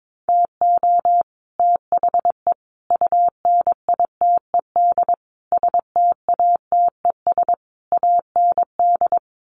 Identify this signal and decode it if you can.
Morse code, heard as a keyed tone, about 22 wpm, 700 Hz. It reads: TO THE UNITED STATES AND
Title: Sets of 5 Words Encore 22wpm